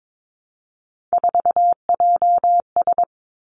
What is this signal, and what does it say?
Morse code, heard as a keyed tone, about 22 wpm, 700 Hz. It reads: 4JS
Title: Three Challenging Letter-Number Combinations 22wpm